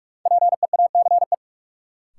Morse code, heard as a keyed tone, about 41 wpm, 700 Hz. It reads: PEACE
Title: Sentences from Top 1500 Words - New Word Drill Alt - 45wpm